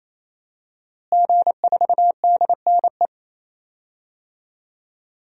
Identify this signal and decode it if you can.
Morse code, heard as a keyed tone, about 28 wpm, 700 Hz. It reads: G4DNE
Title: DXCC-100 Call Signs - 5 Characters - Repeated 3x - 28wpm